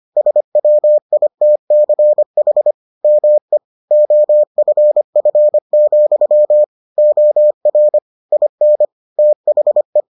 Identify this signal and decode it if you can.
Morse code, heard as a keyed tone, about 25 wpm, 600 Hz. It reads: SWITCH ME OFF, OR IN THE